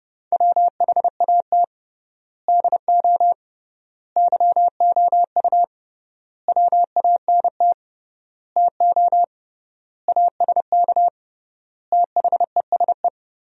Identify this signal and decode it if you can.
Morse code, heard as a keyed tone, about 30 wpm, 700 Hz. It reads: WHAT DO YOU WANT TO ASK THESE
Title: 3x Word-Spacing - Sentences from Top 200 Words - 30wpm